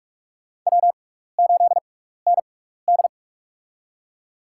Morse code, heard as a keyed tone, about 45 wpm, spaced as for 15 wpm, 700 Hz. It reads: W8ND
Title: USA Call Signs - 4 Characters - 6x Character Spacing - 45wpm